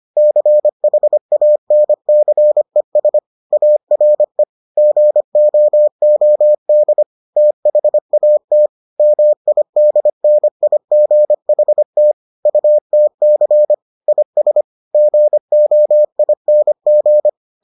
Morse code, heard as a keyed tone, about 25 wpm, 600 Hz. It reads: CHANCES ARE GOOD THAT MIDNIGHT UTC IS GOING